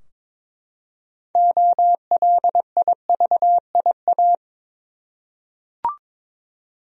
Morse code, heard as a keyed tone, about 22 wpm, 700 Hz. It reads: OLIVIA
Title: USA Common Names - 22wpm